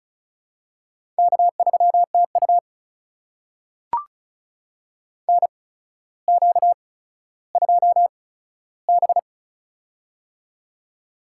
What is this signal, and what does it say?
Morse code, heard as a keyed tone, about 36 wpm, spaced as for 15 wpm, 700 Hz. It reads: K3TU NQ2B
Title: USA Call Signs - 8x Character Spacing - 35wpm